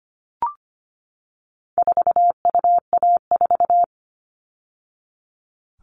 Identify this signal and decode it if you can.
Morse code, heard as a keyed tone, about 25 wpm, 700 Hz. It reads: 4UA4
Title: All Cut Numbers - 4-Digit 25wpm